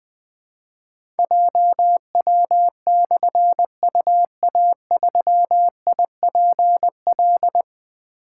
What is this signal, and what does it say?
Morse code, heard as a keyed tone, about 20 wpm, 700 Hz. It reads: JW/UA3IPL
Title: Difficult Call Signs 20wpm